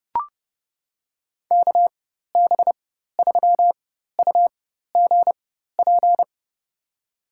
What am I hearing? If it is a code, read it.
Morse code, heard as a keyed tone, about 31 wpm, spaced as for 14 wpm, 700 Hz. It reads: KB3UGP
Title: USA Call Signs - 4x Character Spacing - 30wpm